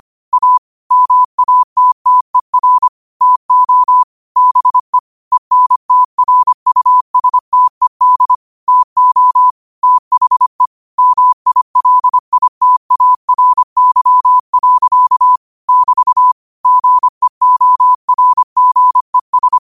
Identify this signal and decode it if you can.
Morse code, heard as a keyed tone, about 25 wpm, 1000 Hz. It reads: A MATTER TO BE ENTRUSTED TO THE MILITARY. <BT> GEORGES
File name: qotdfiles / 25wpm